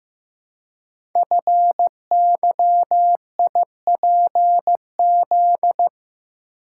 Morse code, heard as a keyed tone, about 15 wpm, 700 Hz. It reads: FYIPZ
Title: Code Groups - 5 Letter:Numbers - 15wpm